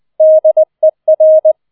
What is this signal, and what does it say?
Morse code, heard as a keyed tone, about 20 wpm, 630 Hz. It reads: DER
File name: 20wpm